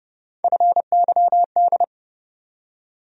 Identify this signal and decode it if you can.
Morse code, heard as a keyed tone, about 30 wpm, 700 Hz. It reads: FYD
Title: USA Call Sign Postfixes - 3 Letters 30wpm